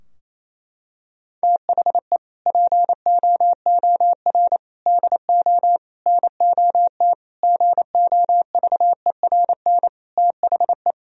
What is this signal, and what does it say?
Morse code, heard as a keyed tone, about 28 wpm, 700 Hz. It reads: THE POOR DO NOT GOVERN THE
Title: Sentences from Top 1000 Words - Review Part 2 - 28wpm